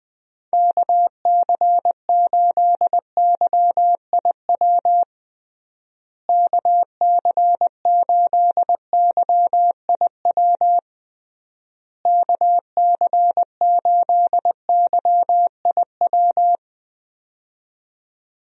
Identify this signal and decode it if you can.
Morse code, heard as a keyed tone, about 21 wpm, 700 Hz. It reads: KC8YIW KC8YIW KC8YIW
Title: USA Call Signs - 6 Characters - Repeated 3x - 20wpm